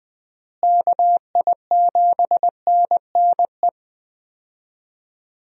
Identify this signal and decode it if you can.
Morse code, heard as a keyed tone, about 20 wpm, 700 Hz. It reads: KI7NNE
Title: USA Call Signs - 6 Characters - Repeated 2x - 20wpm